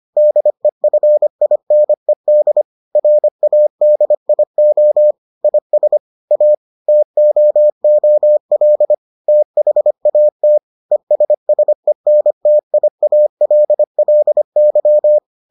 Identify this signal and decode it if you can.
Morse code, heard as a keyed tone, about 25 wpm, 600 Hz. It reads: DEFINED RADIO IS A TOOL THAT ESSENTIALLY